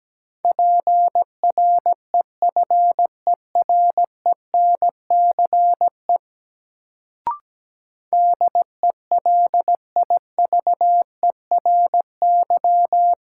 Morse code, heard as a keyed tone, about 17 wpm, 700 Hz. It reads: PREFERENCE DELIVERY
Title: Sentences from Top 1500 Words - New Word Drill Alt - 17wpm